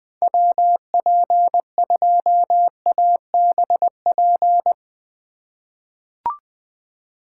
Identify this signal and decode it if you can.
Morse code, heard as a keyed tone, about 20 wpm, 700 Hz. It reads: WP2ABP
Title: USA Call Signs - 6 Characters - 3x Character Spacing - 20wpm